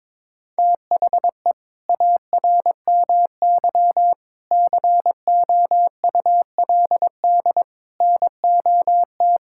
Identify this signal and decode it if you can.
Morse code, heard as a keyed tone, about 22 wpm, 700 Hz. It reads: THE ARMY COULD NOT
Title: Sentences from Top 1000 Words - Review Part 5 - 22wpm